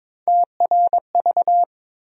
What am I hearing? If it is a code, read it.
Morse code, heard as a keyed tone, about 22 wpm, 700 Hz. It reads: TRV
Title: USA Call Sign Postfixes - 3 Letters 22wpm